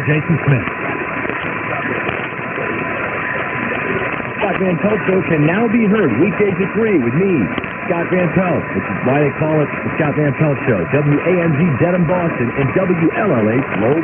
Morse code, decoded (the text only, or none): none